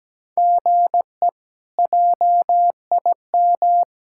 Morse code, heard as a keyed tone, about 17 wpm, 700 Hz. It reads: GE JIM
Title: K1USN SST QSO Practice - 17wpm